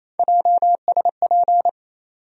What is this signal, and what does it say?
Morse code, heard as a keyed tone, about 28 wpm, 700 Hz. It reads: JSP
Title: USA Call Sign Postfixes - 3 Letters 28wpm